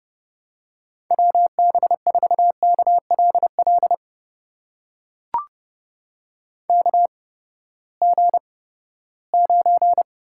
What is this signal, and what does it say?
Morse code, heard as a keyed tone, about 30 wpm, spaced as for 17 wpm, 700 Hz. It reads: WB4KLL KG9
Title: USA Call Signs - 8x Character Spacing - 30wpm